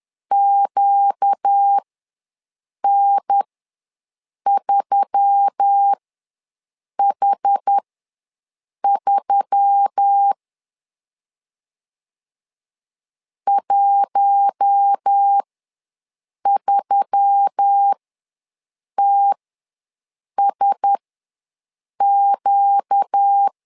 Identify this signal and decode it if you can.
Morse code, heard as a keyed tone, about 10 wpm, spaced as for 6 wpm, 800 Hz. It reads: QN3H3 13TSQ
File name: LEZIONI / 4 Q1S3 ws 10wpm 2437